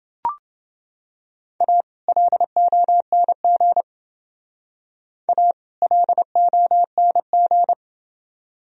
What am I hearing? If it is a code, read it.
Morse code, heard as a keyed tone, about 29 wpm, 700 Hz. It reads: A LONG A LONG